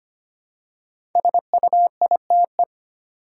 Morse code, heard as a keyed tone, about 25 wpm, 700 Hz. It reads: SUITE
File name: 5 Letter Words - First 3 Letters Given - 25wpm